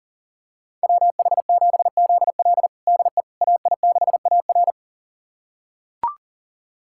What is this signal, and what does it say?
Morse code, heard as a keyed tone, about 40 wpm, 700 Hz. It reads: WH7ZL DE AI6AR
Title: USA Callsign DE Callsign - 5 Characters - 40wpm